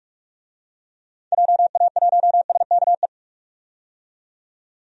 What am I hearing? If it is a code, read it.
Morse code, heard as a keyed tone, about 43 wpm, 690 Hz. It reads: JA1SKE
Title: DXCC-100 Call Signs - 6 Characters - 45wpm